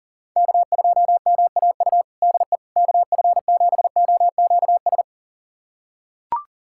Morse code, heard as a keyed tone, about 40 wpm, 700 Hz. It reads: K2MAU DE KF7OQS
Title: USA Callsign DE Callsign 40wpm